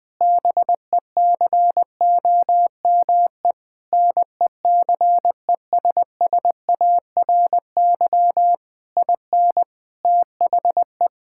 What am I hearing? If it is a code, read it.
Morse code, heard as a keyed tone, about 21 wpm, 700 Hz. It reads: BECOME NECESSARY IN THE
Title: Sentences from Top 1000 Words - Review Part 5 - 20wpm